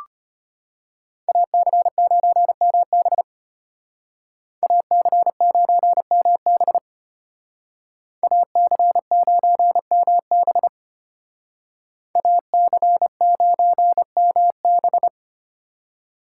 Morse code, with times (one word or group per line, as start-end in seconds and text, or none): 1.3-3.2 AC9MB
4.6-6.8 AC9MB
8.2-10.7 AC9MB
12.2-15.1 AC9MB